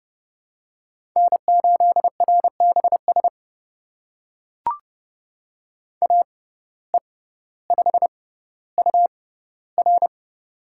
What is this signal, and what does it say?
Morse code, heard as a keyed tone, about 30 wpm, spaced as for 15 wpm, 700 Hz. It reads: N8RBS AE5UR
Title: USA Call Signs - 5 Characters - 6x Character Spacing - 30wpm